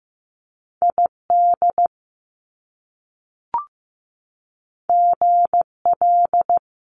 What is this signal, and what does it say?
Morse code, heard as a keyed tone, about 15 wpm, 700 Hz. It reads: ID GL
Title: Top Two Letter Combinations in Words 15wpm